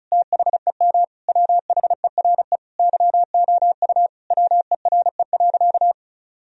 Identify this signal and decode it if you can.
Morse code, heard as a keyed tone, about 36 wpm, 700 Hz. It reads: THEM WHERE YOU WERE.
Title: Sentences from Top 1000 Words - Review Part 4 - 35wpm